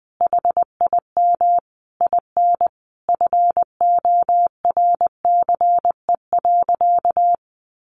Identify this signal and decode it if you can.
Morse code, heard as a keyed tone, about 20 wpm, 700 Hz. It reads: HIM IN FORCE.
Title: Sentences from Top 700 Words 20wpm